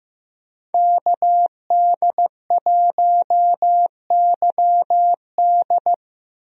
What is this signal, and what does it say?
Morse code, heard as a keyed tone, about 15 wpm, 700 Hz. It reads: KD1YD
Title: USA Call Signs - 5 Characters - 15wpm